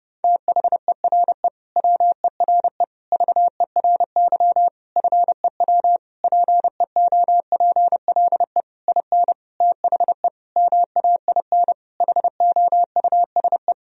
Morse code, heard as a keyed tone, about 30 wpm, 700 Hz. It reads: THERE WERE VERY FEW PEOPLE IN THE MAIN HOUSE
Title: Sentences from Top 1000 Words - Review Part 2 - 30wpm